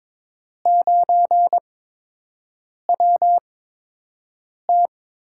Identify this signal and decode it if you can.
Morse code, heard as a keyed tone, about 22 wpm, spaced as for 6 wpm, 700 Hz. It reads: 9WT
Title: USA Call Signs - 4 Characters - 8x Character Spacing - 22wpm